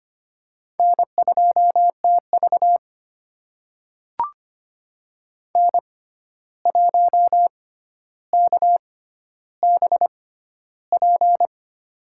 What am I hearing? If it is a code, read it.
Morse code, heard as a keyed tone, about 25 wpm, spaced as for 12 wpm, 700 Hz. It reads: N2TV N1KBP
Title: USA Call Signs - 6x Character Spacing - 25wpm